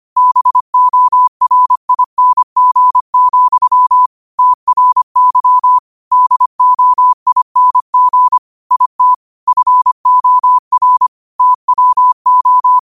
Morse code, heard as a keyed tone, about 25 wpm, 1000 Hz. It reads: BORING, TRY DOING IT FOR TWO
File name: qotdfiles / 25wpm